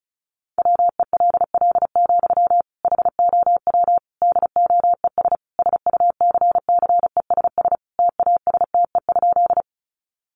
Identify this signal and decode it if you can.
Morse code, heard as a keyed tone, about 34 wpm, 700 Hz. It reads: WELL, HOW DOES SUCCESS TASTE?
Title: Sentences from Top 1100 Words 35wpm